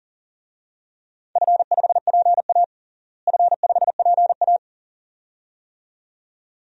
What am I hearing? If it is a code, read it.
Morse code, heard as a keyed tone, about 37 wpm, 700 Hz. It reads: F5PA F5PA
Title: DXCC-100 Call Signs - 4 Characters - Repeated 2x - 40wpm